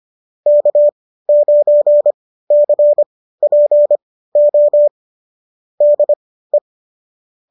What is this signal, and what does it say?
Morse code, heard as a keyed tone, about 25 wpm, spaced as for 15 wpm, 590 Hz. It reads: K9CPO DE
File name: qso211 15